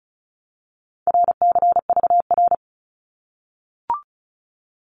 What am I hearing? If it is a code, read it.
Morse code, heard as a keyed tone, about 34 wpm, 700 Hz. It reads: RCVR
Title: Abbreviations Repeated 3x - 35wpm